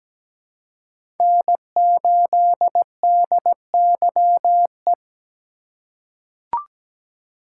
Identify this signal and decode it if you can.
Morse code, heard as a keyed tone, about 17 wpm, 700 Hz. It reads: N8DYE
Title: USA Call Signs - 5 Characters - 2x Character Spacing - 17wpm